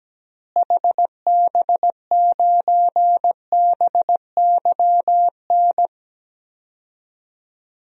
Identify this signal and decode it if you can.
Morse code, heard as a keyed tone, about 17 wpm, 700 Hz. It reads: HB9BYN
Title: DXCC-100 Call Signs - 6 Characters - Repeated 2x - 17wpm